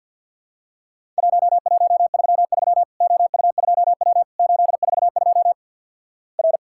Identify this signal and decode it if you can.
Morse code, heard as a keyed tone, about 50 wpm, 700 Hz. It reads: 1133 OU2W 842 R
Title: Worked All Europe DX Contest - QTC Exchange - 50wpm